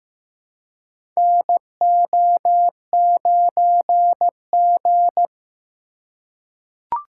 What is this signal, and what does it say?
Morse code, heard as a keyed tone, about 15 wpm, 700 Hz. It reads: NO9G
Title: USA Call Signs - 4 Characters - 2x Character Spacing - 15wpm